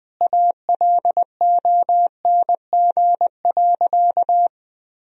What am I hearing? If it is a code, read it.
Morse code, heard as a keyed tone, about 20 wpm, 700 Hz. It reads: ALONG.
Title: Sentences from Top 1000 Words - Review Part 3 - 20wpm